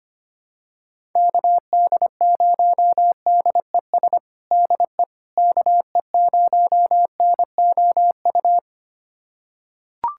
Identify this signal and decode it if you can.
Morse code, heard as a keyed tone, about 25 wpm, 700 Hz. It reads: KD0DES DE KE0NOU
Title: USA Callsign DE Callsign - 6 Characters - 25wpm